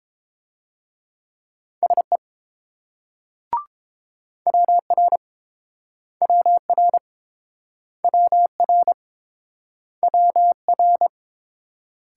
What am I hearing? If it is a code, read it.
Morse code, heard as a keyed tone, about 28 wpm, 690 Hz. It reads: SE WR WR WR WR